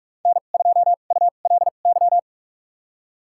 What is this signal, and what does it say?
Morse code, heard as a keyed tone, about 46 wpm, spaced as for 32 wpm, 700 Hz. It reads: N2ULY
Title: USA Call Signs - 5 Characters - 2x Character Spacing - 45wpm